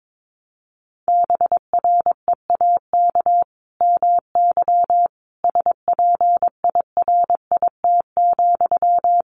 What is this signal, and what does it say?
Morse code, heard as a keyed tone, about 22 wpm, 700 Hz. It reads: BREAK MY SPIRIT,
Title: Sentences from Top 1400 Words 22wpm